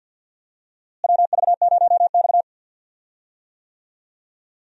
Morse code, heard as a keyed tone, about 49 wpm, 700 Hz. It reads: WV0X
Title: USA Call Signs - 4 Characters - Repeated 2x - 50wpm